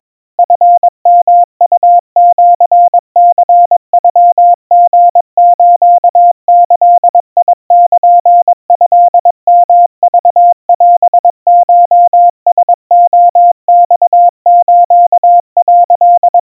Morse code (Japanese Wabun code), heard as a keyed tone, about 22 wpm, 700 Hz. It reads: チヨウシニノリスギルトヨクオコラレマス」
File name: 80 cpm 700 Hz Crispy